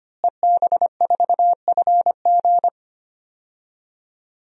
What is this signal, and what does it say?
Morse code, heard as a keyed tone, about 25 wpm, 700 Hz. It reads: EB4FG
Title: DXCC-100 Call Signs - 5 Characters - Repeated 3x - 25wpm